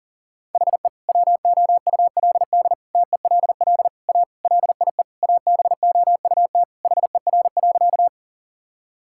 Morse code, heard as a keyed tone, about 40 wpm, 700 Hz. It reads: HE WOULD TELL A LIE ABOUT HER.
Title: Sentences from Top 1600 Words 40wpm